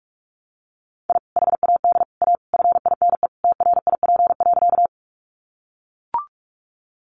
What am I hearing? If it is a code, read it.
Morse code, heard as a keyed tone, about 44 wpm, 700 Hz. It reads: I HAD A FINE TRIP.
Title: Sentences from Top 1300 Words 45wpm